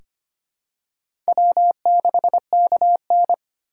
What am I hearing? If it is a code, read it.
Morse code, heard as a keyed tone, about 25 wpm, 700 Hz. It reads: W6KN
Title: USA Call Signs - 4 Characters - 2x Character Spacing - 25wpm